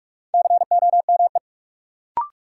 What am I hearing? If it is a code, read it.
Morse code, heard as a keyed tone, about 46 wpm, 700 Hz. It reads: COME
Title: Sets of 2 Words - New Word Drill - 45wpm